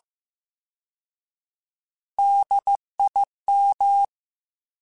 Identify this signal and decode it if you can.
Morse code, heard as a keyed tone, about 15 wpm, 800 Hz. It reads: DIM
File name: Alphabetical / Alphabetical mp3s / Ds @13